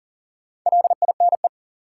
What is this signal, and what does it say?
Morse code, heard as a keyed tone, about 40 wpm, 700 Hz. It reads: LINE
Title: Sentences from Top 1000 Words - Review Part 1 - New Word Drill - 40wpm